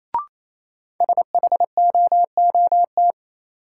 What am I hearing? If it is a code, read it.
Morse code, heard as a keyed tone, about 28 wpm, 700 Hz. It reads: SHOOT
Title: Sentences from Top 1200 Words - New Word Drill Alt - 28wpm